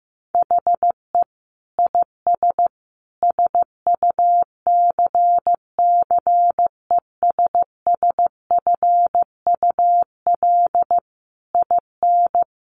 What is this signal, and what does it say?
Morse code, heard as a keyed tone, about 15 wpm, 700 Hz. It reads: HE IS SUCCESSFUL IN
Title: Sentences from Top 900 Words 15wpm